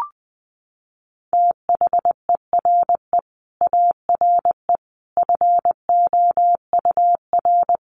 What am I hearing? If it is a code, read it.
Morse code, heard as a keyed tone, about 19 wpm, 700 Hz. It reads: THERE ARE FOUR
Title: Sentences from Top 400 Words 20wpm